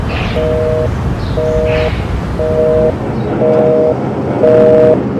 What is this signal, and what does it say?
Busy tone (North America), 480 + 620 Hz, with noise.